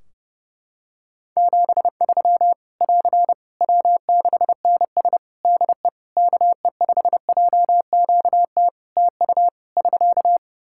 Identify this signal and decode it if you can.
Morse code, heard as a keyed tone, about 30 wpm, 700 Hz. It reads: 73 <AR> W6NS DE KE5JQT TU <SK>
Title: Practice QSO - Exchange 4 - Ending - 30wpm